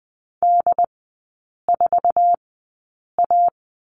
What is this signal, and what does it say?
Morse code, heard as a keyed tone, about 20 wpm, spaced as for 8 wpm, 700 Hz. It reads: D4A
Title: 067 Lesson 12e - Letter d - Three Characters 20wpm